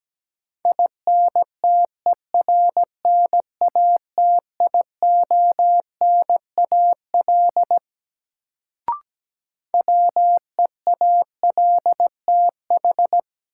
Morse code, heard as a keyed tone, about 17 wpm, 700 Hz. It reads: INTERNATIONAL WEALTH
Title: Sentences from Top 1000 Words - Review Part 4 - New Word Drill - 17wpm